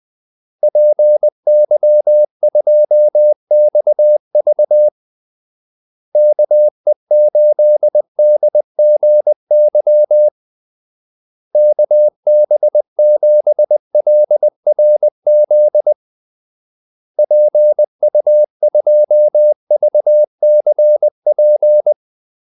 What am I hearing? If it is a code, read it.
Morse code, heard as a keyed tone, about 20 wpm, 600 Hz. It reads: PY2XV KE8DGY KB7LRZ PU2VCP
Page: Kaldesignaler 20 wpm